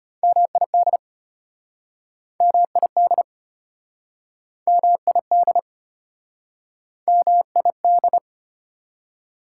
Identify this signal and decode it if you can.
Morse code, heard as a keyed tone, about 35 wpm, 700 Hz. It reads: MID MID MID MID